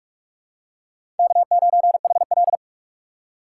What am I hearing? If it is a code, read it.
Morse code, heard as a keyed tone, about 46 wpm, 700 Hz. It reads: K9HL
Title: USA Call Signs - 4 Characters - 6x Character Spacing - 45wpm